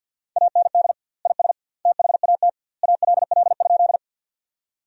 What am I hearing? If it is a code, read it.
Morse code, heard as a keyed tone, about 50 wpm, 700 Hz. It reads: AND IS THAT ALL?
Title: 2x Word-Spacing - Sentences from Top 100 Words - 50wpm